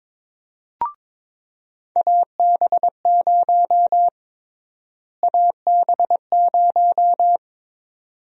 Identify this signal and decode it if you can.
Morse code, heard as a keyed tone, about 22 wpm, 700 Hz. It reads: AB0 AB0
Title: USA Call Sign Prefixes - Repeated 2x - 22wpm